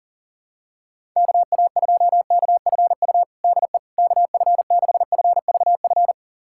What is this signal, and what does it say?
Morse code, heard as a keyed tone, about 40 wpm, 700 Hz. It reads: KA2KFU DE KF6FVF
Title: USA Callsign DE Callsign - 6 Characters - 40wpm